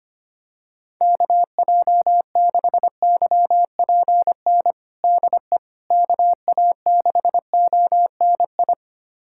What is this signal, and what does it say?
Morse code, heard as a keyed tone, about 25 wpm, 700 Hz. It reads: KJ6YPN DE KA6ONI
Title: USA Callsign DE Callsign - 6 Characters - 25wpm